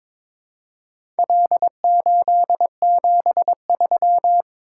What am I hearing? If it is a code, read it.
Morse code, heard as a keyed tone, about 21 wpm, 700 Hz. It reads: L873
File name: practice / 22wpm / Four Letter-Number Combinations - 22wpm